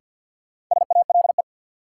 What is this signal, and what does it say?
Morse code, heard as a keyed tone, about 51 wpm, 700 Hz. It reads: SALE
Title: Sentences from Top 1400 Words - New Word Drill - 50wpm